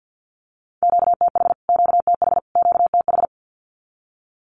Morse code, heard as a keyed tone, about 50 wpm, 700 Hz. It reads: QTH QTH QTH
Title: Q-Codes Repeated 3x 50wpm